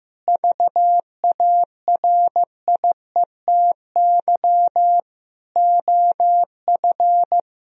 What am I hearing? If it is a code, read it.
Morse code, heard as a keyed tone, about 15 wpm, 700 Hz. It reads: VARIETY OF
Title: Sentences from Top 1000 Words - Review Part 4 - 15wpm